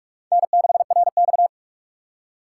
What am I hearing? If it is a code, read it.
Morse code, heard as a keyed tone, about 40 wpm, 700 Hz. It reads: N6RX